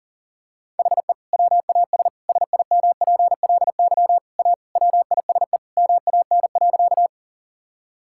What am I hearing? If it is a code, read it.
Morse code, heard as a keyed tone, about 40 wpm, 700 Hz. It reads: HE WAS SIMPLY A WISE MAN.
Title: Sentences from Top 1600 Words 40wpm